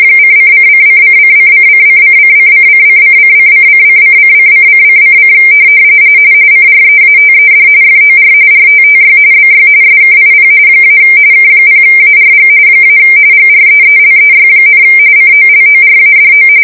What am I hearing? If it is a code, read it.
RTTY, about 45.45 baud. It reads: ::::::::::::::::::::::::::::::: CREATED FOR RTTY BY SKIP - K0ARG - GREELEY, COLORADO